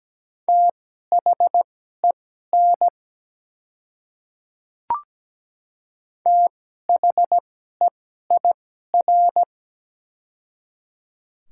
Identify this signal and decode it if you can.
Morse code, heard as a keyed tone, about 17 wpm, spaced as for 12 wpm, 700 Hz. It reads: THEN THEIR
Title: 2x Character Spacing - Top 100 Words - 17wpm